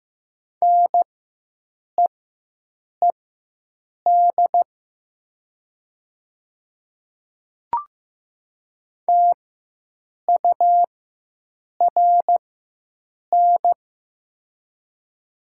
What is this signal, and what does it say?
Morse code, heard as a keyed tone, about 15 wpm, spaced as for 7 wpm, 700 Hz. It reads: NEED TURN
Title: 4x Character Spacing - Top 300 Words - 15wpm